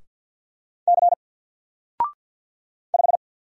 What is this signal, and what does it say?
Morse code, heard as a keyed tone, about 50 wpm, 700 Hz. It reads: C 5
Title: Single Letter-Number 50wpm